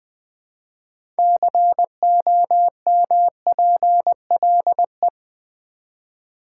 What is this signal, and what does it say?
Morse code, heard as a keyed tone, about 20 wpm, 700 Hz. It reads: COMPLE
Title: Build A Word - 2 Letters At A Time - 8-Letter Words - 20wpm